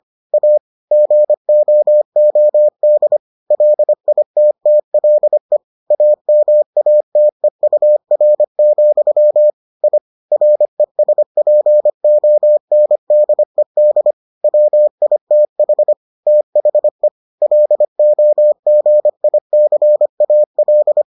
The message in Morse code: A GOOD LITTLE AMATEUR, I RESPONDED WITH THE LOGICAL

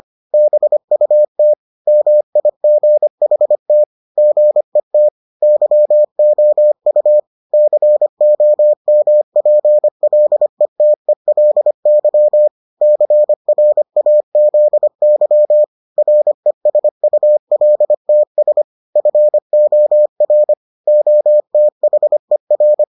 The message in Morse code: BUT MIGHT GET YOU COMPLETELY CRAZY RESULTS FOR OTHER

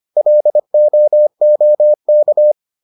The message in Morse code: LOOK

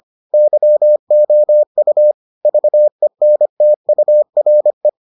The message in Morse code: YOU VENTURE

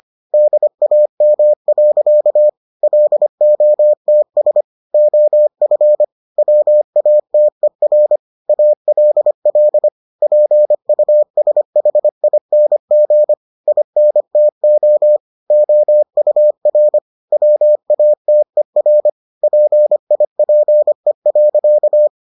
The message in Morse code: DAM. LOTS OF WATER ALL PUSHING INTO OUR WATER PIPE.

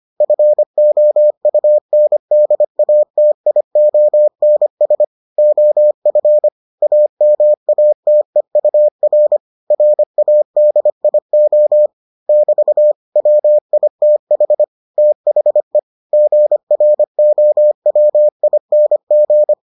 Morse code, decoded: FOUNDATIONS OF AMATEUR RADIO <BT> WITH THE GROWING

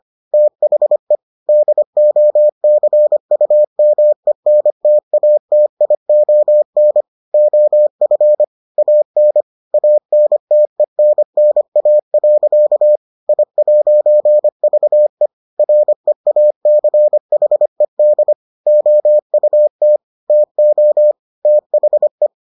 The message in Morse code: THE DOCUMENTATION OF AN ANTENNA. I'VE REACHED OUT TO THE

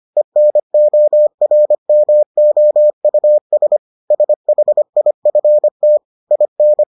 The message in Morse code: ENORMOUS SHIFT IN

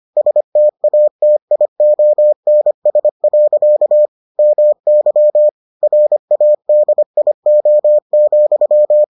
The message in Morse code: STATIONS. MY RADIO,

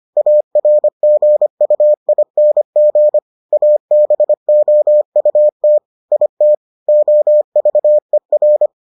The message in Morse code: ARGUING ABOUT IT OVER

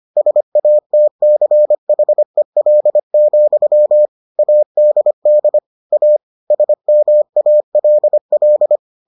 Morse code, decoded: SATCHEL, ADD A SMALL

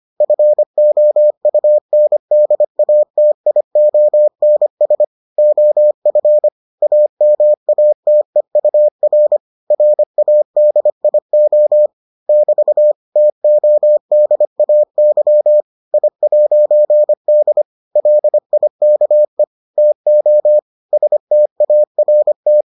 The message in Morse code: FOUNDATIONS OF AMATEUR RADIO <BT> TODAY I'D LIKE TO START

